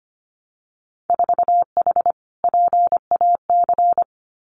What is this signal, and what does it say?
Morse code, heard as a keyed tone, about 25 wpm, 700 Hz. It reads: 4H PAC
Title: Winter Field Day Exchanges Encore 25wpm